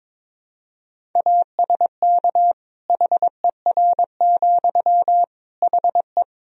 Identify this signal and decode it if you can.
Morse code, heard as a keyed tone, about 22 wpm, 700 Hz. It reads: ASK HER, HE
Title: Sentences from Top 200 Words and Abbreviations 22wpm